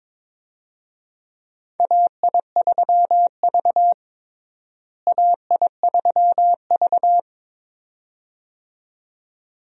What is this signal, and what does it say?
Morse code, heard as a keyed tone, about 21 wpm, 700 Hz. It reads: AI3V AI3V
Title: USA Call Signs - 4 Characters - Repeated 2x - 22wpm